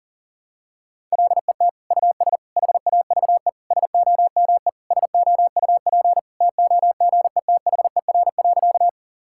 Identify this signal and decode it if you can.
Morse code, heard as a keyed tone, about 40 wpm, 700 Hz. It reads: LET US HAVE SOME SOUP TOGETHER.
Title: Sentences from Top 1000 Words - Review Part 6 - 40wpm